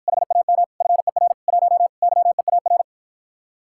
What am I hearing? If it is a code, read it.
Morse code, heard as a keyed tone, about 52 wpm, 700 Hz. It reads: HAM FER 1 YEAR